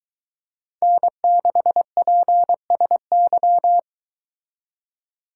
Morse code, heard as a keyed tone, about 23 wpm, 700 Hz. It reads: N6PSY